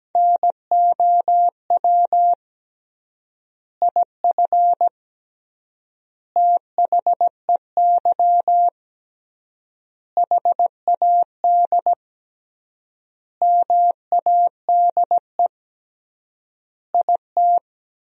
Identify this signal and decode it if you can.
Morse code, heard as a keyed tone, about 17 wpm, 700 Hz. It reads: NOW IF THEY HAD MADE IT
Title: 3x Word-Spacing - Sentences from Top 500 Words - 17wpm